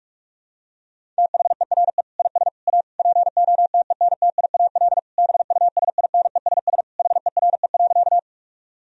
Morse code, heard as a keyed tone, about 46 wpm, 700 Hz. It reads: THERE IS A POTENTIAL BUSINESS HERE.
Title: Sentences from Top 1000 Words - Review Part 4 - 45wpm